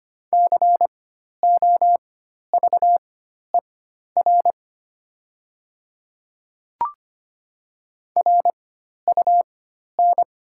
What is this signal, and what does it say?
Morse code, heard as a keyed tone, about 25 wpm, spaced as for 12 wpm, 700 Hz. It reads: COVER RUN
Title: 4x Character Spacing - Top 300 Words - 25wpm